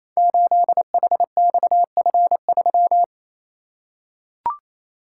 Morse code, heard as a keyed tone, about 28 wpm, 700 Hz. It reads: 8HXF3
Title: Code Groups - 5 Letter:Numbers - 28wpm